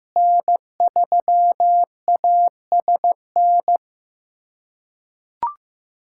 Morse code, heard as a keyed tone, about 15 wpm, 700 Hz. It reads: N3ASN
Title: USA Callsign DE Callsign - 5 Characters - 15wpm